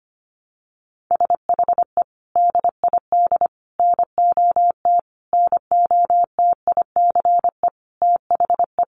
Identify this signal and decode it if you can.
Morse code, heard as a keyed tone, about 25 wpm, 700 Hz. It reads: SHE DID NOT NOTICE THE
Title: Sentences from Top 1100 Words 25wpm